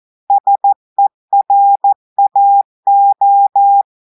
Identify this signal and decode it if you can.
Morse code, heard as a keyed tone, about 14 wpm, 800 Hz. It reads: SERAO
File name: audios / portugais / portugais texte en clair 2 vitesse 14 mots minutes